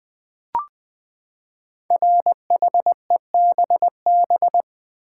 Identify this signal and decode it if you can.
Morse code, heard as a keyed tone, about 20 wpm, 700 Hz. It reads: RHEBB
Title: Code Groups - 5 Letters - 20wpm